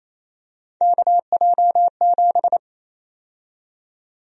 Morse code, heard as a keyed tone, about 28 wpm, 700 Hz. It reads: KJ7
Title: USA Call Sign Prefixes - 28wpm